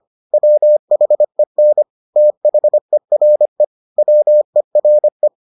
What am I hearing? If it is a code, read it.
Morse code, heard as a keyed tone, about 25 wpm, 600 Hz. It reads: WHEN THERE WERE